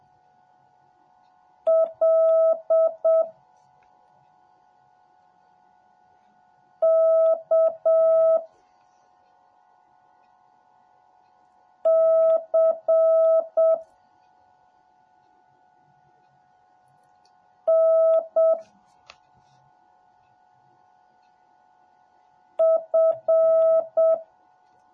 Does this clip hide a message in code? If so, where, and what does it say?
Yes: Morse code from 1.7 to 24.1 s, heard as a keyed tone, about 7 wpm, 650 Hz. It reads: L K C N F